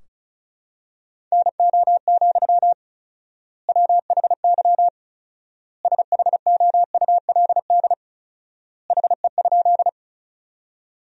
Morse code, heard as a keyed tone, about 36 wpm, 700 Hz. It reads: NO, WHY SHOULD HE?
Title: 4x Word-Spacing - Sentences from Top 300 Words - 35wpm